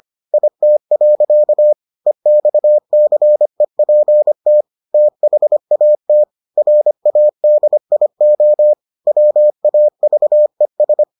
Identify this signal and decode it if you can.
Morse code, heard as a keyed tone, about 25 wpm, 600 Hz. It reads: IT. EXCEPT THAT RADIO WAVES